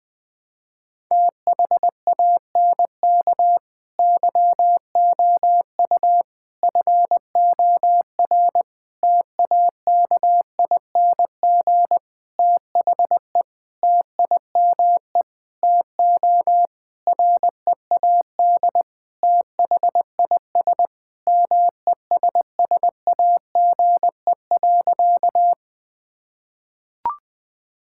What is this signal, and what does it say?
Morse code, heard as a keyed tone, about 20 wpm, 700 Hz. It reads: THANK YOU FOR TAKING THE TIME TO READ THIS MESSAGE.
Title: Sentences with 9-Word Phrases - 20wpm